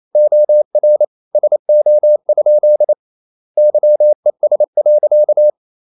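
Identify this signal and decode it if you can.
Morse code, heard as a keyed tone, about 28 wpm, 600 Hz. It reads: OR SO? YES.
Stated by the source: Morse code practice at 28 wpm, text of Ken Follett's novel Never in English, from CW med Gnister.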